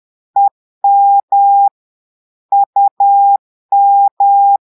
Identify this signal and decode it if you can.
Morse code, heard as a keyed tone, about 10 wpm, 800 Hz. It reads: EM UM
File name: portugais texte en clair 2 vitesse 10 mots minutes